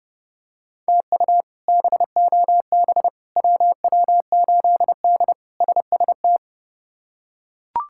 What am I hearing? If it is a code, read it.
Morse code, heard as a keyed tone, about 30 wpm, 700 Hz. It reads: TU BOB WW8D SST
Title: K1USN SST QSO Practice - 30wpm